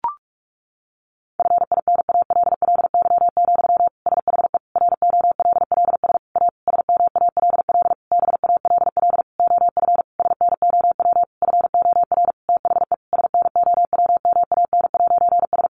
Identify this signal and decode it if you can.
Morse code, heard as a keyed tone, about 44 wpm, 700 Hz. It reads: FINALLY, SHE ROLLS A SMALL BALL OF SNOW FOR THE SNOWMAN'S
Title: Easy Phrases 45wpm